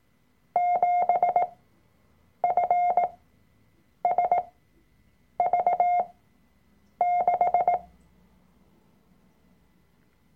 Morse code, heard as a keyed tone, about 18 wpm, spaced as for 7 wpm, 710 Hz. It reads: 7FSV6
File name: Workshop Dateien / CW-Online-Kurs / Lektion 003